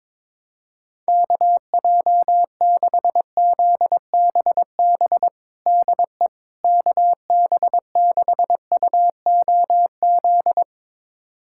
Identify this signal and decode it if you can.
Morse code, heard as a keyed tone, about 22 wpm, 700 Hz. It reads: KJ6ZBB DE KB6UOZ
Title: USA Callsign DE Callsign 22wpm